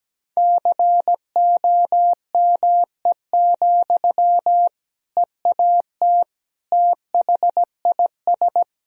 Morse code, heard as a keyed tone, about 17 wpm, 700 Hz. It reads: COME, EAT THIS
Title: Sentences from Top 1000 Words - Review Part 1 - 17wpm